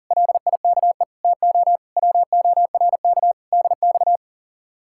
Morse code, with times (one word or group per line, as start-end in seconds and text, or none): 0.1-1.0 LIKE
1.2-1.8 TO
2.0-3.3 WORK
3.5-4.2 DX